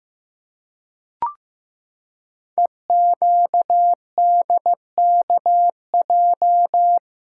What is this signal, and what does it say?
Morse code, heard as a keyed tone, about 15 wpm, 700 Hz. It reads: EQDKJ
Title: Code Groups - 5 Letters - 15wpm